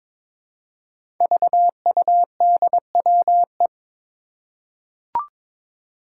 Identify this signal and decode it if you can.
Morse code, heard as a keyed tone, about 22 wpm, 700 Hz. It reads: VUDWE
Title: Five Challenging Letter-Number Combinations 22wpm